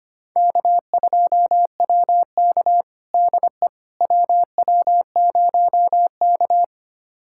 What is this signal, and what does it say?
Morse code, heard as a keyed tone, about 25 wpm, 700 Hz. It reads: K2WK DE WW0K
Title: USA Callsign DE Callsign - 4 Characters - 25wpm